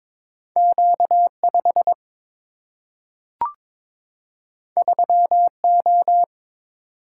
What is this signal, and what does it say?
Morse code, heard as a keyed tone, about 22 wpm, 700 Hz. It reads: Q 5 3 O
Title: Single-Letter Single-Number 22wpm